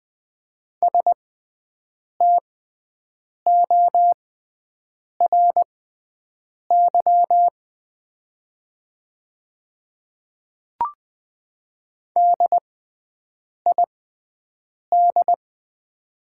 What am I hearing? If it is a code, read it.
Morse code, heard as a keyed tone, about 20 wpm, spaced as for 7 wpm, 700 Hz. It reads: STORY DID
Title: 6x Character Spacing - Top 300 Words - 20wpm